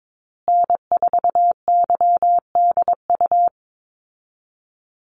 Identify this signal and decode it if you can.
Morse code, heard as a keyed tone, about 22 wpm, 700 Hz. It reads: N4YDU
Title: Call Signs Repeated 3x - 22wpm